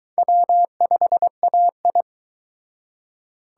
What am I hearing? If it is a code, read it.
Morse code, heard as a keyed tone, about 22 wpm, 700 Hz. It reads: W5AI